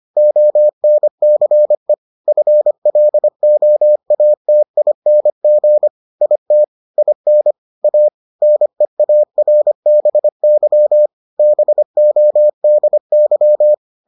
Morse code, heard as a keyed tone, about 25 wpm, 600 Hz. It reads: ONCE FLOATING IT IN A NEARBY BODY